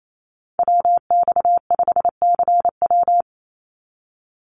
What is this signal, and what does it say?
Morse code, heard as a keyed tone, about 28 wpm, 700 Hz. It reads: WX5CW
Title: Call Signs Repeated 3x - 28wpm